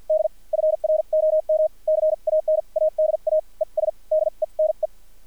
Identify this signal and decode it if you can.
Morse code, heard as a keyed tone, about 50 wpm, spaced as for 40 wpm, 620 Hz. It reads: D FROM CANADA ES DENE
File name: archived / CRSnet